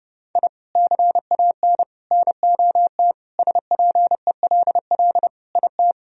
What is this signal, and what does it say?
Morse code, heard as a keyed tone, about 30 wpm, 700 Hz. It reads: I CAN NOT SPELL IT
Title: Sentences from Top 1000 Words - Review Part 1 - 30wpm